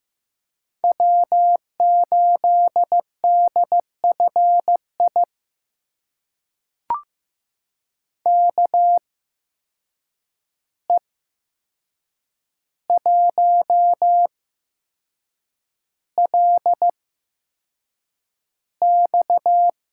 Morse code, heard as a keyed tone, about 15 wpm, spaced as for 6 wpm, 700 Hz. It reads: W8DFI KE1LX
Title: USA Call Signs - 5 Characters - 8x Character Spacing - 15wpm